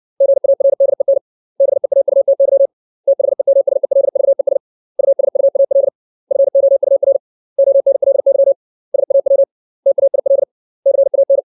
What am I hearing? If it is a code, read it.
Morse code, heard as a keyed tone, about 60 wpm, 550 Hz. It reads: GARDEN BEAUTY THEMSELVES USUAL FOUR ONLY SAW TAIL CAN
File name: 632 words / 632 Words 60WPM